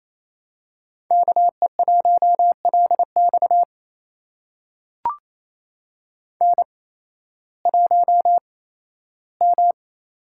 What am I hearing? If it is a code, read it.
Morse code, heard as a keyed tone, about 28 wpm, spaced as for 15 wpm, 700 Hz. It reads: KE1LX N1M
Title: USA Call Signs - 5 Characters - 8x Character Spacing - 28wpm